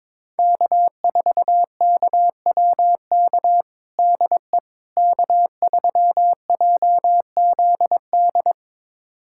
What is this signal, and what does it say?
Morse code, heard as a keyed tone, about 22 wpm, 700 Hz. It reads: K4KWK DE K3JZD
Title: USA Callsign DE Callsign - 5 Characters - 22wpm